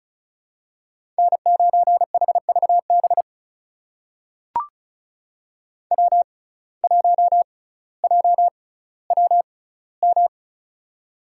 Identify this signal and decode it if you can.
Morse code, heard as a keyed tone, about 36 wpm, spaced as for 18 wpm, 700 Hz. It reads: N9HVB W1JWM
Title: USA Call Signs - 6x Character Spacing - 35wpm